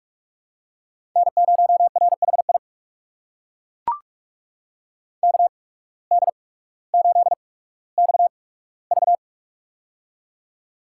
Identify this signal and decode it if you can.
Morse code, heard as a keyed tone, about 41 wpm, spaced as for 19 wpm, 700 Hz. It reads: N0RHI KD8XV
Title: USA Call Signs - 5 Characters - 8x Character Spacing - 45wpm